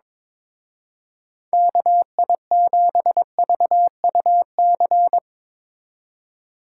Morse code, heard as a keyed tone, about 23 wpm, 700 Hz. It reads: KI7VUC
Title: Speed Racing - USA 2x3 Call Signs 22wpm